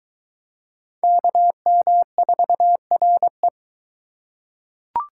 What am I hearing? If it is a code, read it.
Morse code, heard as a keyed tone, about 23 wpm, 700 Hz. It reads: KM4RE